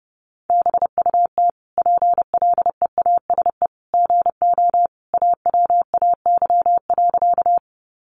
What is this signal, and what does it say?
Morse code, heard as a keyed tone, about 30 wpm, 700 Hz. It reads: BUT PLEASE GO AWAY.
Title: Sentences from Top 1500 Words 30wpm